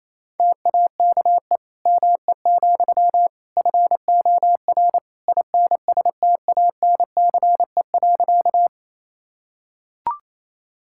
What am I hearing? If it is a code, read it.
Morse code, heard as a keyed tone, about 28 wpm, 700 Hz. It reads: TAKE ME, FOR INSTANCE.
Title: Sentences from Top 1000 Words - Review Part 5 - 28wpm